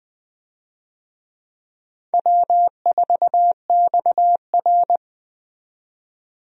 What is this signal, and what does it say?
Morse code, heard as a keyed tone, about 21 wpm, 700 Hz. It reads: W4XR
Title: USA Call Signs - 4 Characters - 20wpm